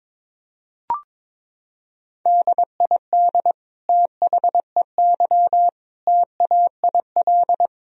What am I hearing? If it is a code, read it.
Morse code, heard as a keyed tone, about 22 wpm, 700 Hz. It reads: DID THEY TAIL
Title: Sentences from Top 1000 Words - Review Part 3 - 22wpm